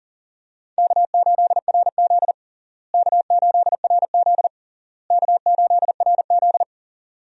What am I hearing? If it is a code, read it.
Morse code, heard as a keyed tone, about 37 wpm, 700 Hz. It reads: K8RZ K8RZ K8RZ
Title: USA Call Signs - 4 Characters - Repeated 3x - 40wpm